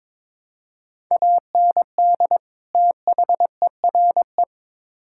Morse code, heard as a keyed tone, about 22 wpm, 700 Hz. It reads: AND THERE
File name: Build A Phrase - 2-Word Phrases - 22wpm